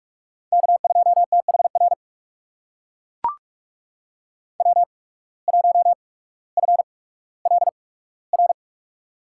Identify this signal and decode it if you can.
Morse code, heard as a keyed tone, about 43 wpm, spaced as for 19 wpm, 700 Hz. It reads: K2THR W1FLR
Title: USA Call Signs - 5 Characters - 8x Character Spacing - 45wpm